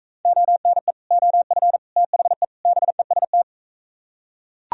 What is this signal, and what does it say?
Morse code, heard as a keyed tone, about 42 wpm, 700 Hz. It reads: ONE OF THE BEST